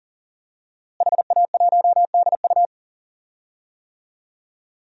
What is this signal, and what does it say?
Morse code, heard as a keyed tone, about 40 wpm, 690 Hz. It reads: HA1DU
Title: DXCC-100 Call Signs - 5 Characters - Repeated 2x - 40wpm